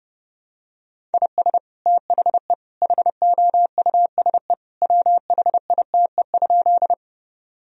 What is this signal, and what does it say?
Morse code, heard as a keyed tone, about 30 wpm, 700 Hz. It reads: IS THE HOUSE WHITE?
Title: Sentences from Top 1000 Words - Review Part 1 - 30wpm